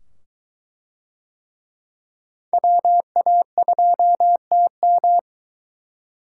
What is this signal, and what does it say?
Morse code, heard as a keyed tone, about 23 wpm, 700 Hz. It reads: WA2TM